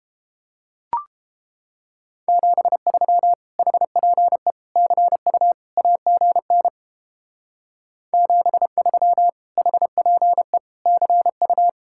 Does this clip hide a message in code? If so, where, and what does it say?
Yes: Morse code from 2.3 to 11.7 s, heard as a keyed tone, about 29 wpm, 700 Hz. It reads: 73 HPE CU AGN 73 HPE CU